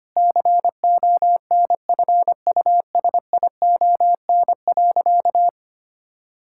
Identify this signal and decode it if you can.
Morse code, heard as a keyed tone, about 25 wpm, 700 Hz. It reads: CONFUSION.
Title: Sentences from Top 1600 Words 25wpm